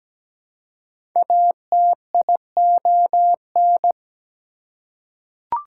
Morse code, heard as a keyed tone, about 17 wpm, 700 Hz. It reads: ATION
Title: Prefixes and Suffixes 17wpm